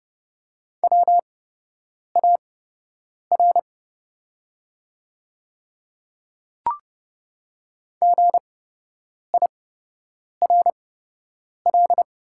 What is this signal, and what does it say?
Morse code, heard as a keyed tone, about 30 wpm, spaced as for 8 wpm, 700 Hz. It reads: WAR GIRL
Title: 8x Character Spacing - Top 500 Words - 30wpm